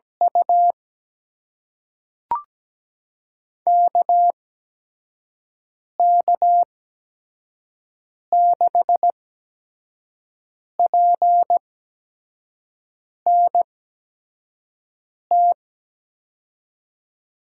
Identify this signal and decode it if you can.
Morse code, heard as a keyed tone, about 17 wpm, spaced as for 5 wpm, 700 Hz. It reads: U KK6PNT
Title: USA Call Signs - 8x Character Spacing - 17wpm